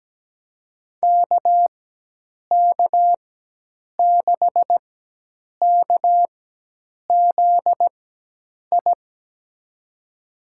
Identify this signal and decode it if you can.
Morse code, heard as a keyed tone, about 17 wpm, spaced as for 8 wpm, 700 Hz. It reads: KK6KZI
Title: USA Call Signs - 6 Characters - 4x Character Spacing - 17wpm